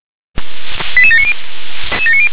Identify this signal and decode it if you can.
Five-tone selective call (ZVEI), heard as quick runs of five tones, tones of about 70 ms. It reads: 88790